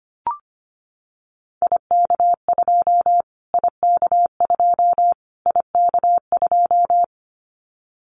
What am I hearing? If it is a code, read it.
Morse code, heard as a keyed tone, about 25 wpm, 700 Hz. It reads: IK2 IK2 IK2
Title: Callsign Prefixes for Top 25 DXCC - Repeated 3x - 25wpm